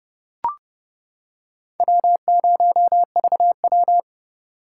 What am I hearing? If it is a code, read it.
Morse code, heard as a keyed tone, about 29 wpm, 700 Hz. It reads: W0VW